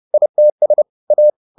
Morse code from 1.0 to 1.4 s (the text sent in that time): A